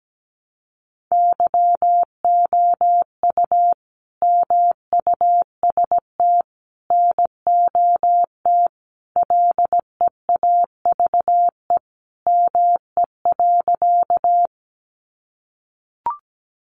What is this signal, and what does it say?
Morse code, heard as a keyed tone, about 17 wpm, 700 Hz. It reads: YOU MUST NOT LEAVE ME.
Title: Sentences from Top 600 Words 17wpm